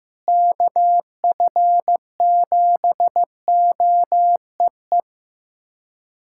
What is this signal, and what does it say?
Morse code, heard as a keyed tone, about 15 wpm, 700 Hz. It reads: KF7OEE
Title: USA Callsign DE Callsign - 6 Characters - 15wpm